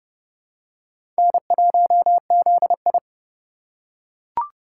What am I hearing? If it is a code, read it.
Morse code, heard as a keyed tone, about 30 wpm, 700 Hz. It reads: N1ZI
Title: USA Call Signs - 4 Characters - 2x Character Spacing - 30wpm